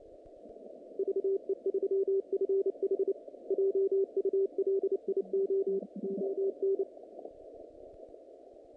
Morse code, heard as a keyed tone, about 29 wpm, 390 Hz. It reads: VE3FH JULIO ON
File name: contests / naqp-cw-k5tr-op-k5pi- -2100